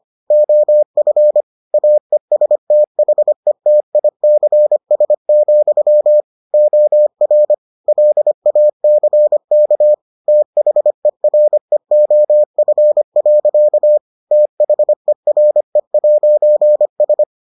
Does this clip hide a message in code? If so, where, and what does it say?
Yes: Morse code from 0.3 to 17.2 s, heard as a keyed tone, about 25 wpm, 600 Hz. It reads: OF AESTHETICS, OR LACK THEREOF. THERE'S